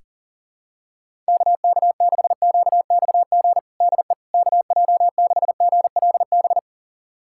Morse code, heard as a keyed tone, about 37 wpm, 700 Hz. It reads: KK6QXG DE KJ6GLB
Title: USA Callsign DE Callsign - 6 Characters - 40wpm